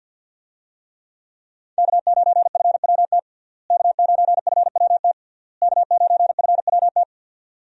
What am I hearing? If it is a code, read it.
Morse code, heard as a keyed tone, about 51 wpm, 700 Hz. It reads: K9FWT K9FWT K9FWT
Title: USA Call Signs - 5 Characters - Repeated 3x - 50wpm